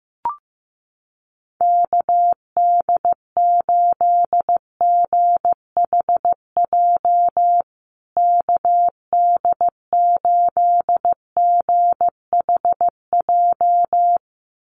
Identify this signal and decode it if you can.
Morse code, heard as a keyed tone, about 15 wpm, 700 Hz. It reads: KD8GHJ KD8GHJ
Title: USA Call Signs Repeated 2x 15wpm